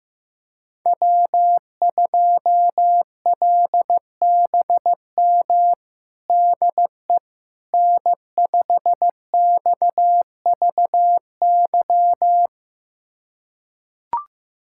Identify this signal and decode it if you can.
Morse code, heard as a keyed tone, about 15 wpm, 700 Hz. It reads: W2LBM DE N5XVY
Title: USA Callsign DE Callsign 15wpm